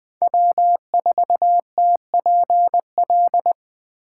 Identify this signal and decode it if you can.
Morse code, heard as a keyed tone, about 20 wpm, 700 Hz. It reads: W4TPL
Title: USA Call Signs - 5 Characters - 8x Character Spacing - 20wpm